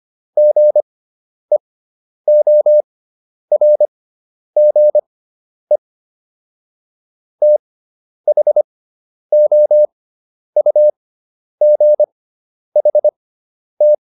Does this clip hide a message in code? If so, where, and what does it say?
Yes: Morse code from 0.4 to 13.9 s, heard as a keyed tone, about 25 wpm, spaced as for 10 wpm, 600 Hz. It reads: GEORGE THOUGHT